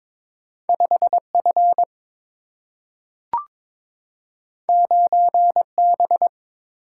Morse code, heard as a keyed tone, about 23 wpm, 700 Hz. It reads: 5 F 9 B
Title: Single-Letter Single-Number 22wpm